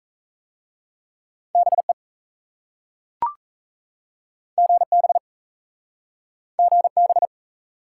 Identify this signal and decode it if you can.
Morse code, heard as a keyed tone, about 37 wpm, 700 Hz. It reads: BE GB GB